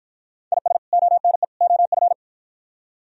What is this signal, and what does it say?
Morse code, heard as a keyed tone, about 52 wpm, 690 Hz. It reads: IS ONE OF